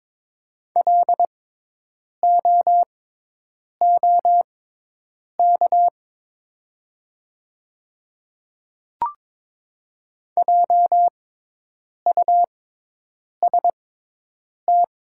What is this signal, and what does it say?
Morse code, heard as a keyed tone, about 22 wpm, spaced as for 8 wpm, 700 Hz. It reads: LOOK JUST
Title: 6x Character Spacing - Top 200 Words - 22wpm